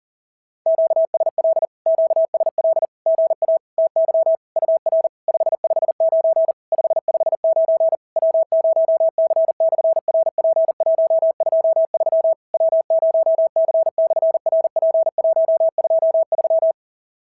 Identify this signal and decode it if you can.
Morse code, heard as a keyed tone, about 38 wpm, 650 Hz. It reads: QSL QSL GA TY UR 559 559 W0C/RP123 W0C/RP123
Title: SOTA QSO Practice 40wpm